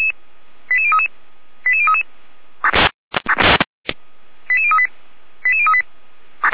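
Five-tone selective call (ZVEI), heard as quick runs of five tones, tones of about 70 ms. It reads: 88033 88033 88038 88038
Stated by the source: ZVEI five-tone selective call received by SDR.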